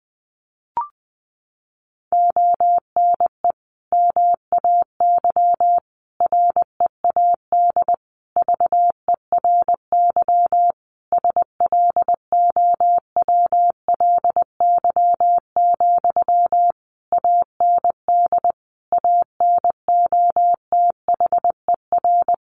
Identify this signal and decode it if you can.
Morse code, heard as a keyed tone, about 20 wpm, 700 Hz. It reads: ONE MAY READ VERY SLOWLY, AND ANOTHER
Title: Sentences from Top 1100 Words 20wpm